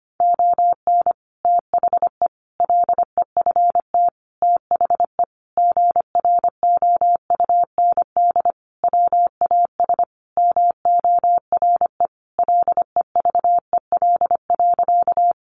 Morse code, heard as a keyed tone, about 25 wpm, 700 Hz. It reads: ON THE LEFT THE GROUND WAS MORE LEVEL.
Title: Sentences from Top 700 Words 25wpm